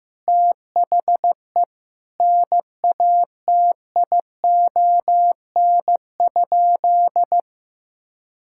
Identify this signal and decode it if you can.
Morse code, heard as a keyed tone, about 15 wpm, 700 Hz. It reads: THE NATION?
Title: Sentences from Top 1000 Words - Review Part 6 - 15wpm